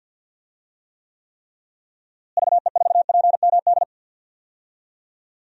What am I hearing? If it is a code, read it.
Morse code, heard as a keyed tone, about 52 wpm, 700 Hz. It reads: VE4PMD
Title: DXCC-100 Call Signs - 6 Characters - Repeated 3x - 50wpm